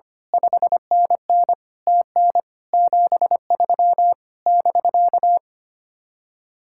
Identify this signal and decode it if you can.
Morse code, heard as a keyed tone, about 24 wpm, 700 Hz. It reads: 5NN TN 73 <BK>
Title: Speed-Racing Common QSO Phrases - 25wpm